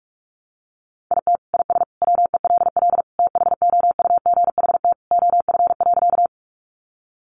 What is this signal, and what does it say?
Morse code, heard as a keyed tone, about 44 wpm, 700 Hz. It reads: IT IS WELL THOUGHT OF.
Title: Sentences from Top 300 Words 45wpm